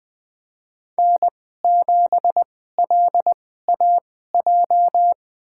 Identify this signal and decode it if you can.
Morse code, heard as a keyed tone, about 20 wpm, spaced as for 14 wpm, 700 Hz. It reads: N7LAJ
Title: USA Call Signs - 5 Characters - 2x Character Spacing - 20wpm